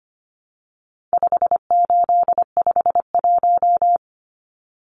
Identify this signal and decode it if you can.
Morse code, heard as a keyed tone, about 25 wpm, 700 Hz. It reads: 5851
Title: Numbers - 4-Digit 25wpm